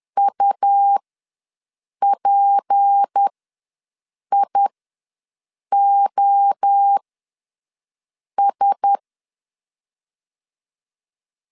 Morse code, heard as a keyed tone, about 10 wpm, spaced as for 6 wpm, 800 Hz. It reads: UPIOS
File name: LEZIONI / 10 SIGNS / 10C DOT ws 10wpm 4213